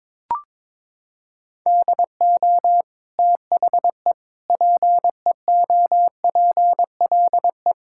Morse code, heard as a keyed tone, about 22 wpm, 700 Hz. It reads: DO THE PEOPLE
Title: Sentences from Top 1000 Words - Review Part 5 - 22wpm